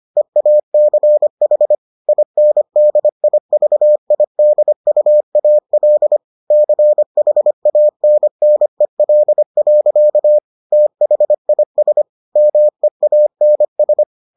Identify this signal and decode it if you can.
Morse code, heard as a keyed tone, about 25 wpm, 600 Hz. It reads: EACH INDIVIDUAL CHANNEL. THIS MEANS